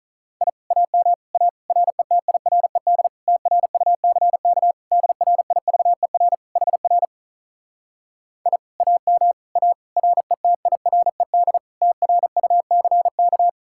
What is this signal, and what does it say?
Morse code, heard as a keyed tone, about 48 wpm, 700 Hz. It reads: I AM A RETIRED TRUCK DRIVER HR I AM A RETIRED TRUCK